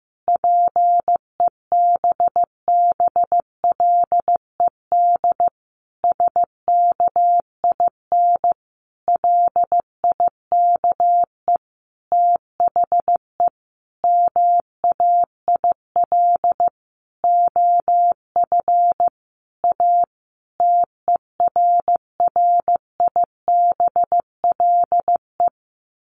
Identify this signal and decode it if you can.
Morse code, heard as a keyed tone, about 15 wpm, 700 Hz. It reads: PEBBLED SKIN LIKE THE MAIL OF A TERRIBLE
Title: A Sound Of Thunder 15wpm